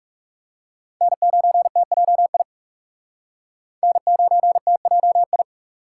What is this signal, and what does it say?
Morse code, heard as a keyed tone, about 40 wpm, 700 Hz. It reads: N9TJI N9TJI